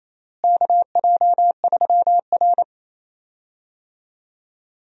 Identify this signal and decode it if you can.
Morse code, heard as a keyed tone, about 27 wpm, 700 Hz. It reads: KJ3R
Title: USA Call Signs - 4 Characters - Repeated 3x - 28wpm